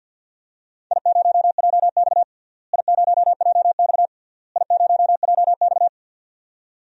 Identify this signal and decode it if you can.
Morse code, heard as a keyed tone, about 48 wpm, 700 Hz. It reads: I0JX I0JX I0JX
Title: DXCC-100 Call Signs - 4 Characters - Repeated 3x - 50wpm